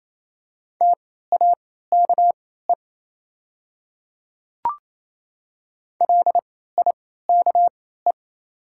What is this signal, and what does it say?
Morse code, heard as a keyed tone, about 27 wpm, spaced as for 16 wpm, 690 Hz. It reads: TAKE LIKE
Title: 3x Character Spacing - Top 200 Words - 28wpm